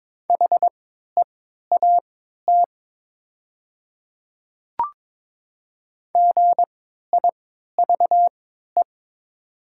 Morse code, heard as a keyed tone, about 22 wpm, spaced as for 12 wpm, 700 Hz. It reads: HEAT GIVE
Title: 3x Character Spacing - Top 500 Words - 22wpm